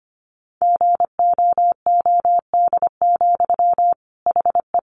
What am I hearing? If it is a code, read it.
Morse code, heard as a keyed tone, about 25 wpm, 700 Hz. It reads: GOOD, HE
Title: Sentences from Top 600 Words 25wpm